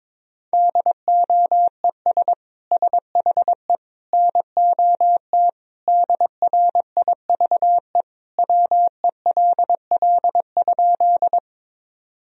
Morse code, heard as a keyed tone, about 22 wpm, 700 Hz. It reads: DOES SHE NOT DRIVE WELL?
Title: Sentences from Top 1000 Words - Review Part 2 - 22wpm